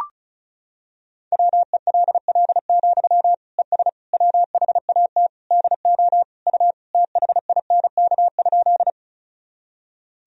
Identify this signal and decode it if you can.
Morse code, heard as a keyed tone, about 36 wpm, 700 Hz. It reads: WELL, ES WHAT DO U THINK?
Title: Sentences from Top 200 Words and Abbreviations 35wpm